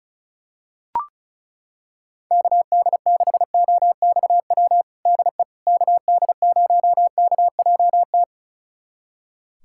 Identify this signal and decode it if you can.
Morse code, heard as a keyed tone, about 36 wpm, 700 Hz. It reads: KD6OXW DE KD0KJT
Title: USA Callsign DE Callsign - 6 Characters - 35wpm